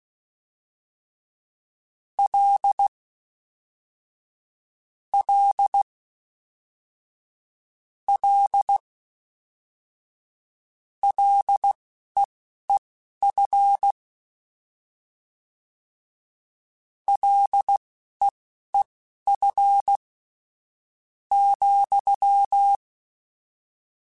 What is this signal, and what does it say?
Morse code, heard as a keyed tone, about 16 wpm, spaced as for 11 wpm, 800 Hz. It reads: L L L LEEF LEEF ,
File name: Short Runs / Short Runs-mp3s / SinglePrefixes @10